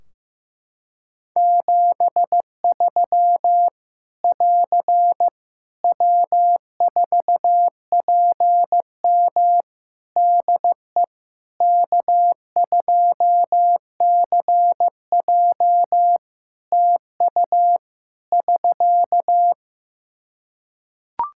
Morse code, heard as a keyed tone, about 15 wpm, 700 Hz. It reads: 73 <AR> W4PM DE K2CJ TU <SK>
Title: K1USN SST QSO Practice 15wpm